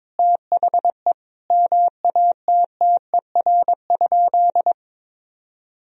Morse code, heard as a keyed tone, about 22 wpm, 700 Hz. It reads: THE MATTER?
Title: Sentences from Top 1000 Words - Review Part 4 - 22wpm